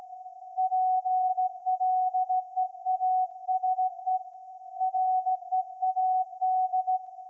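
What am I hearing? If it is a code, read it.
Morse code, heard as a keyed tone, about 16 wpm, 740 Hz. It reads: PLEASE READ
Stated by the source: Morse code from SAQ, received at a station in Putignano, Apulia, Italy, on VLF.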